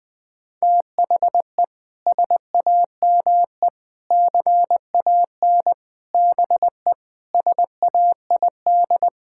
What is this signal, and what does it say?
Morse code, heard as a keyed tone, about 20 wpm, 700 Hz. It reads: THE SAME CAN BE SAID
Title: Sentences with 6-Word Phrases - 20wpm